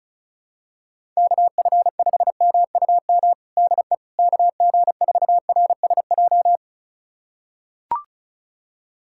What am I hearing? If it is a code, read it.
Morse code, heard as a keyed tone, about 34 wpm, 700 Hz. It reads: KF5MUM DE KG4RSJ
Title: USA Callsign DE Callsign - 6 Characters - 35wpm